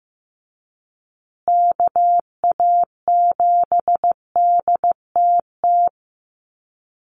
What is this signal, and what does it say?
Morse code, heard as a keyed tone, about 15 wpm, 700 Hz. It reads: KA7DTT
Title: USA Call Signs Repeated 2x 15wpm